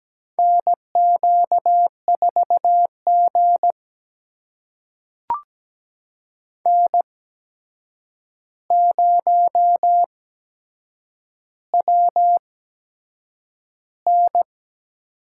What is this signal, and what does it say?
Morse code, heard as a keyed tone, about 17 wpm, spaced as for 7 wpm, 700 Hz. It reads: NQ4G N0WN
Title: USA Call Signs - 4 Characters - 8x Character Spacing - 17wpm